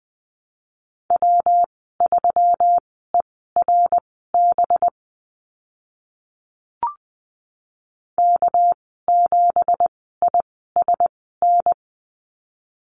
Morse code, heard as a keyed tone, about 20 wpm, spaced as for 15 wpm, 700 Hz. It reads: W3ERB K7ISN
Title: 207 Lesson 33g - Letter j - Call Signs 20wpm